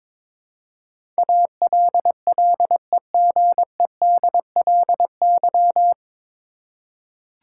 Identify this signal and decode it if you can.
Morse code, heard as a keyed tone, about 22 wpm, 700 Hz. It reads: ALLEGEDLY
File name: 9 Letter Words - First 3 Letters Given - 22wpm